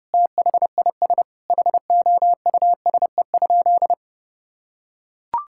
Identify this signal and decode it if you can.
Morse code, heard as a keyed tone, about 30 wpm, 700 Hz. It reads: THIS HOUSE?
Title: Sentences from Top 1600 Words 30wpm